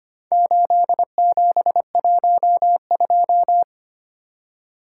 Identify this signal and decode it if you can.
Morse code, heard as a keyed tone, about 25 wpm, 700 Hz. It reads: 8712
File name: practice / 25wpm / Four Letter-Number Combinations - 25wpm